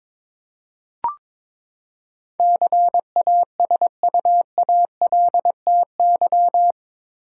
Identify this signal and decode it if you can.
Morse code, heard as a keyed tone, about 22 wpm, 700 Hz. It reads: CASUALTY
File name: practice / 22wpm / Eight Letter Words - 22wpm